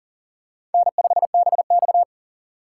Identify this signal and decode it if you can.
Morse code, heard as a keyed tone, about 40 wpm, 700 Hz. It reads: N5BX
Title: USA Call Signs - 4x Character Spacing - 40wpm